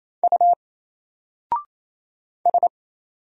Morse code, heard as a keyed tone, about 28 wpm, 700 Hz. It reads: U S
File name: Single-Letter-Number-28wpm